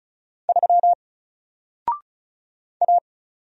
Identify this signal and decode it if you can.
Morse code, heard as a keyed tone, about 36 wpm, 700 Hz. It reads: 3 A
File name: Single-Letter-Number-35wpm